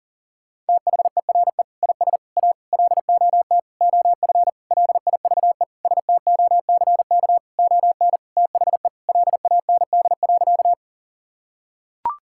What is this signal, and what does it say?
Morse code, heard as a keyed tone, about 40 wpm, 700 Hz. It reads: THERE IS A LOT OF LIVESTOCK ON THE LAND.
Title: Sentences from Top 1000 Words - Review Part 5 - 40wpm